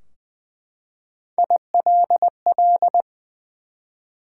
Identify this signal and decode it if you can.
Morse code, heard as a keyed tone, about 20 wpm, 700 Hz. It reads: ILL
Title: Three Letter Words - 20wpm